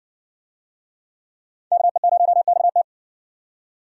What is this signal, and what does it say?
Morse code, heard as a keyed tone, about 57 wpm, 700 Hz. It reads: KE0XT